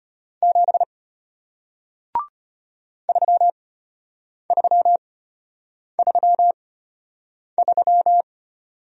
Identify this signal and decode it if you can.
Morse code, heard as a keyed tone, about 34 wpm, spaced as for 8 wpm, 700 Hz. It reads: 7 3333